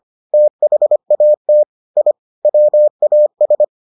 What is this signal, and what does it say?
Morse code, heard as a keyed tone, about 25 wpm, 600 Hz. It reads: THAT I WAS